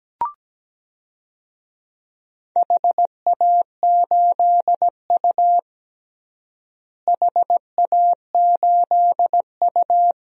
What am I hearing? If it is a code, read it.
Morse code, heard as a keyed tone, about 17 wpm, 700 Hz. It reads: HA8U HA8U
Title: DXCC-100 Call Signs - 4 Characters - Repeated 3x - 17wpm